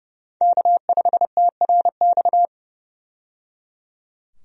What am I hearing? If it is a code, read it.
Morse code, heard as a keyed tone, about 30 wpm, 700 Hz. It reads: K5TRX
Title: USA Call Signs - 5 Characters - 30wpm